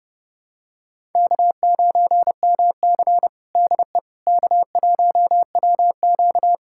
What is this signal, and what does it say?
Morse code, heard as a keyed tone, about 32 wpm, 700 Hz. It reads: K9MC DE K1WQ
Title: USA Callsign DE Callsign - 4 Characters - 30wpm